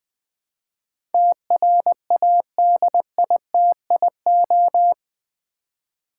Morse code, heard as a keyed tone, about 20 wpm, 700 Hz. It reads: TRADITIO
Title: Build A Word - 4 Letters At A Time - 11-Letter Words - 20wpm